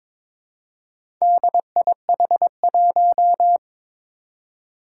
Morse code, heard as a keyed tone, about 21 wpm, 700 Hz. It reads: DIH1
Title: Four Challenging Letter-Number Combinations 22wpm